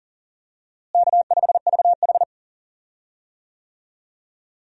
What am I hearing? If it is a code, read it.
Morse code, heard as a keyed tone, about 39 wpm, 700 Hz. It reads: K5VH
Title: USA Call Signs - 4 Characters - Repeated 3x - 40wpm